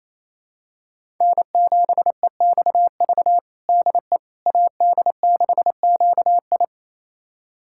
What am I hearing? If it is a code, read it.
Morse code, heard as a keyed tone, about 28 wpm, 700 Hz. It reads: N7EXV DE AD6QI
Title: USA Callsign DE Callsign - 5 Characters - 28wpm